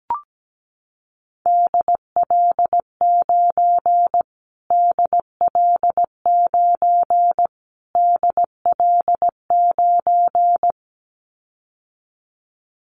Morse code, heard as a keyed tone, about 17 wpm, 700 Hz. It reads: DL9 DL9 DL9
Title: Callsign Prefixes for Top 25 DXCC - Repeated 3x - 17wpm